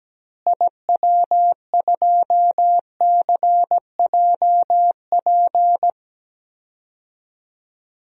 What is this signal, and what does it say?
Morse code, heard as a keyed tone, about 17 wpm, 700 Hz. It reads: IW2CJP
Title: DXCC-100 Call Signs - 6 Characters - Repeated 3x - 17wpm